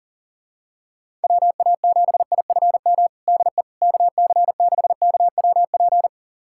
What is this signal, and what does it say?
Morse code, heard as a keyed tone, about 40 wpm, 700 Hz. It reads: WA7IFM DE KC6KWP
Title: USA Callsign DE Callsign - 6 Characters - 40wpm